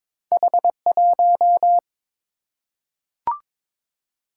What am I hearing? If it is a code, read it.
Morse code, heard as a keyed tone, about 22 wpm, 700 Hz. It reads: H1
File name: practice / 22wpm / Two Letter-Number Combinations - 22wpm